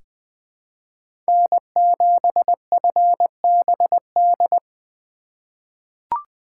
Morse code, heard as a keyed tone, about 20 wpm, 700 Hz. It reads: N7FBD
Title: USA Call Signs - 5 Characters - 3x Character Spacing - 20wpm